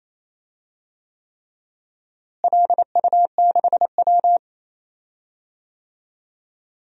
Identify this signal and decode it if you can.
Morse code, heard as a keyed tone, about 28 wpm, 700 Hz. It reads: LU6W
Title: DXCC-100 Call Signs - 4 Characters - Repeated 3x - 28wpm